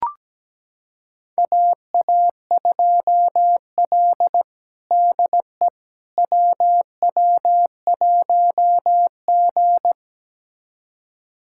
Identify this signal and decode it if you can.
Morse code, heard as a keyed tone, about 17 wpm, 700 Hz. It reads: AA2L DE WW1G
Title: USA Callsign DE Callsign - 4 Characters - 17wpm